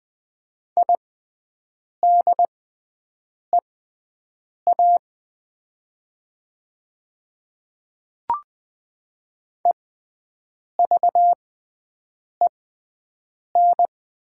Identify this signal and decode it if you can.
Morse code, heard as a keyed tone, about 20 wpm, spaced as for 7 wpm, 700 Hz. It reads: IDEA EVEN
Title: 6x Character Spacing - Top 500 Words - 20wpm